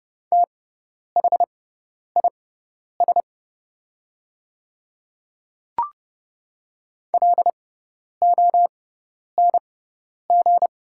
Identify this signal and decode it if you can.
Morse code, heard as a keyed tone, about 30 wpm, spaced as for 10 wpm, 700 Hz. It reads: THIS LONG
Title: 6x Character Spacing - Top 100 Words - 30wpm